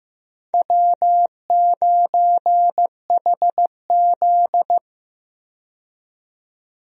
Morse code, heard as a keyed tone, about 15 wpm, 700 Hz. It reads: W9HZ
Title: USA Call Signs - 4 Characters - 15wpm